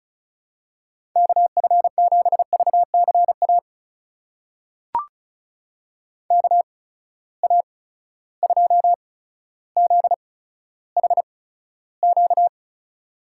Morse code, heard as a keyed tone, about 36 wpm, spaced as for 15 wpm, 700 Hz. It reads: KF7VCA KA2ZHQ
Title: USA Call Signs - 6 Characters - 8x Character Spacing - 35wpm